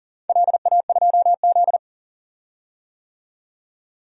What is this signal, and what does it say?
Morse code, heard as a keyed tone, about 40 wpm, 700 Hz. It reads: LA2Z
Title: DXCC-100 Call Signs - 4 Characters - 40wpm